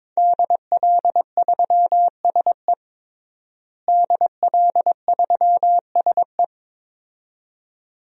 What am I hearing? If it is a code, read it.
Morse code, heard as a keyed tone, about 22 wpm, 700 Hz. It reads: DL3SE DL3SE
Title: DXCC-100 Call Signs - 5 Characters - Repeated 2x - 22wpm